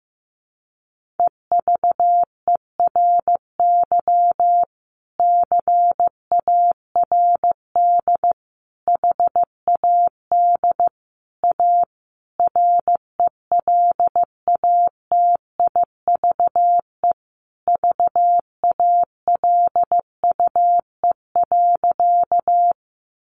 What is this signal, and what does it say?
Morse code, heard as a keyed tone, about 15 wpm, 700 Hz. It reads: EVERY CARD HAD A RELATIVE VALUE.
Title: Sentences from Top 1400 Words 15wpm